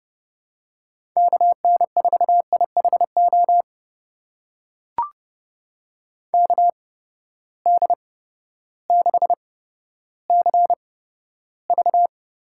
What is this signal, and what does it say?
Morse code, heard as a keyed tone, about 31 wpm, spaced as for 14 wpm, 700 Hz. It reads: KN4IHO KD6CV
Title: USA Call Signs - 6 Characters - 8x Character Spacing - 30wpm